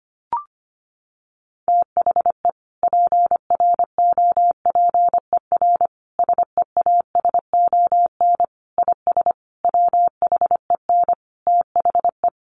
Morse code, heard as a keyed tone, about 25 wpm, 700 Hz. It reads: THE PROPER SEASON IS WHEN THE
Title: Sentences from Top 1400 Words 25wpm